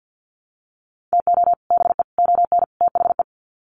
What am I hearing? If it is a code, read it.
Morse code, heard as a keyed tone, about 50 wpm, 700 Hz. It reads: TO BE ON THE
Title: Sets of 4 Words 50wpm